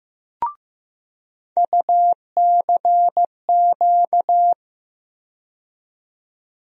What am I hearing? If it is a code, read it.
Morse code, heard as a keyed tone, about 15 wpm, 700 Hz. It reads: UCQ
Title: USA Call Sign Postfixes - 3 Letters 15wpm